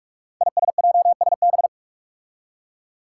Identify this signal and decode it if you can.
Morse code, heard as a keyed tone, about 41 wpm, 700 Hz. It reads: ISJSB
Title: Five Challenging Letter-Number Combinations 45wpm